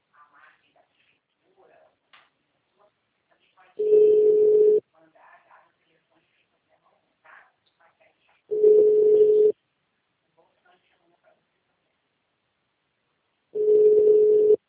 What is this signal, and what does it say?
Ringback tone (Europe, 425 Hz), with speech.
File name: out- -2031- - - .16196